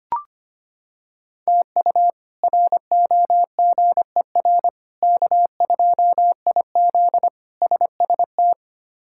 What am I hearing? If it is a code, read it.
Morse code, heard as a keyed tone, about 25 wpm, 700 Hz. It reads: TU ROGER K2IZ SST
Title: K1USN SST QSO Practice - 25wpm